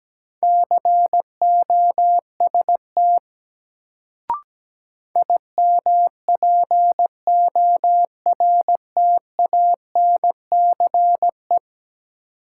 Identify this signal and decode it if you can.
Morse code, heard as a keyed tone, about 17 wpm, 700 Hz. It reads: COST IMPORTANCE
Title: Sets of 3 Words - New Word Drill - 17wpm